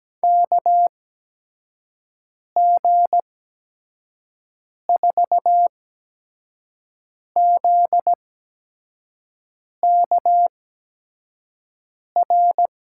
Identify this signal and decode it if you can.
Morse code, heard as a keyed tone, about 17 wpm, spaced as for 5 wpm, 700 Hz. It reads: KG4ZKR
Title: USA Call Signs - 6 Characters - 8x Character Spacing - 17wpm